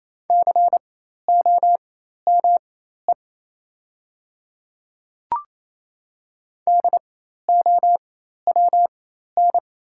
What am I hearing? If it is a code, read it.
Morse code, heard as a keyed tone, about 28 wpm, spaced as for 13 wpm, 700 Hz. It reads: COME DOWN
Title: 4x Character Spacing - Top 100 Words - 28wpm